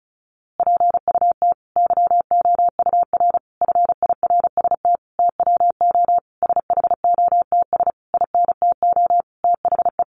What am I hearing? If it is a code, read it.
Morse code, heard as a keyed tone, about 34 wpm, 700 Hz. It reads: PUT YOUR FIRST TWO SHOTS INTO THE
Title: A Sound Of Thunder 35wpm